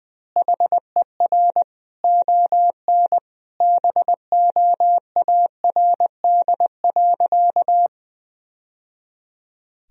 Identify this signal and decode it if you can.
Morse code, heard as a keyed tone, about 20 wpm, 700 Hz. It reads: HER ON BOARD.
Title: Sentences from Top 1000 Words - Review Part 4 - 20wpm